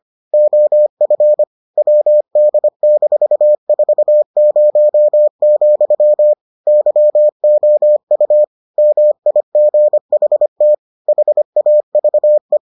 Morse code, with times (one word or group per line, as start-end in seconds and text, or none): 0.3-1.4 OF
1.8-6.3 WD-40,
6.7-8.4 YOU
8.8-10.7 MIGHT
11.1-12.6 HAVE